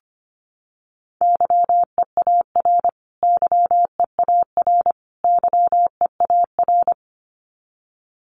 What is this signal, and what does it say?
Morse code, heard as a keyed tone, about 25 wpm, 700 Hz. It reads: YEAR YEAR YEAR
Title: Top 500 Words Repeated 25wpm